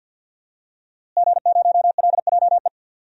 Morse code, heard as a keyed tone, about 51 wpm, 700 Hz. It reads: G0LJE
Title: DXCC-100 Call Signs - 5 Characters - Repeated 2x - 50wpm